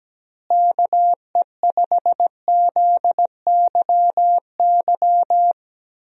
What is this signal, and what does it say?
Morse code, heard as a keyed tone, about 17 wpm, 700 Hz. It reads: KE5ZYY
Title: USA Call Signs - 6 Characters - Repeated 2x - 17wpm